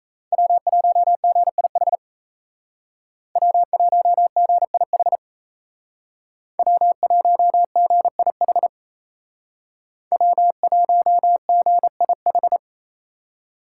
Morse code, heard as a keyed tone, about 36 wpm, 700 Hz. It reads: W1GIH W1GIH W1GIH W1GIH